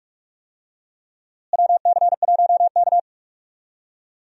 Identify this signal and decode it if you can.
Morse code, heard as a keyed tone, about 45 wpm, 700 Hz. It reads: WC1K
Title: USA Call Signs - 4 Characters - Repeated 2x - 45wpm